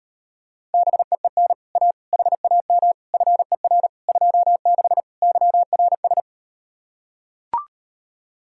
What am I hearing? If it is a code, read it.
Morse code, heard as a keyed tone, about 38 wpm, 700 Hz. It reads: BEEN A HAM FER 26 YRS